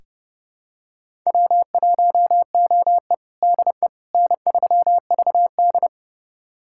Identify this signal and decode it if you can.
Morse code, heard as a keyed tone, about 30 wpm, 700 Hz. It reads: W1OE DE N3VD
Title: USA Callsign DE Callsign - 4 Characters - 30wpm